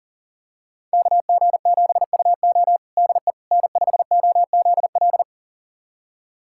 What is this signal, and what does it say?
Morse code, heard as a keyed tone, about 40 wpm, 700 Hz. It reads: KG7UO DE N5OZL
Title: USA Callsign DE Callsign - 5 Characters - 40wpm